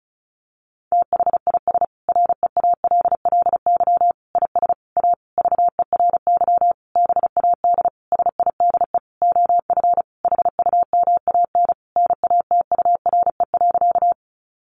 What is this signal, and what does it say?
Morse code, heard as a keyed tone, about 34 wpm, 700 Hz. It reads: THIS REALLY IS A VERY BAD SIDE OF HUMAN NATURE.
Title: Sentences from Top 700 Words 35wpm